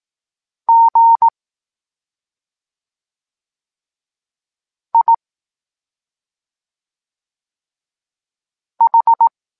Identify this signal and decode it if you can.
Morse code, heard as a keyed tone, about 18 wpm, 920 Hz. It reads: G I H
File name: Tekst-23 4